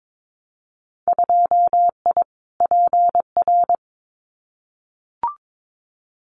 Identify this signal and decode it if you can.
Morse code, heard as a keyed tone, about 22 wpm, 700 Hz. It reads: 2I PR
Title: Winter Field Day Exchanges Encore 22wpm